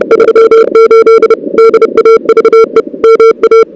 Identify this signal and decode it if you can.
Morse code, heard as a keyed tone, about 31 wpm, 440 Hz. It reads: 38 DAVE MA
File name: CWSprint / Feb2006 / 2006-feb-cw-sprint-k5tr op-n5rz -01:00